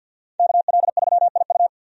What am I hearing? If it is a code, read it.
Morse code, heard as a keyed tone, about 51 wpm, 700 Hz. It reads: KL3IU
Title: USA Call Signs - 5 Characters - Repeated 3x - 50wpm